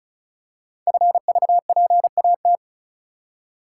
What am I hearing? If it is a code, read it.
Morse code, heard as a keyed tone, about 36 wpm, 690 Hz. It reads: FVPAT
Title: Code Groups - 5 Letters - 35wpm